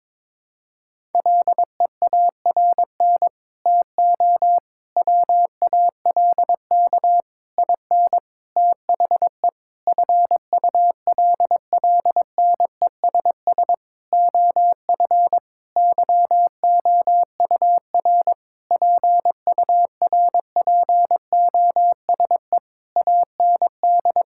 Morse code, heard as a keyed tone, about 22 wpm, 700 Hz. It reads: LEARN TO WALK IN THE FULLNESS OF YOUR PURPOSE AND
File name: Sentences with 9-Word Phrases - 22wpm